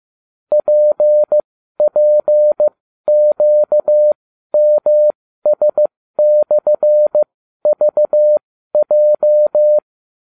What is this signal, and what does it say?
Morse code, heard as a keyed tone, about 15 wpm, spaced as for 12 wpm, 600 Hz. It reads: PPQMS/VJ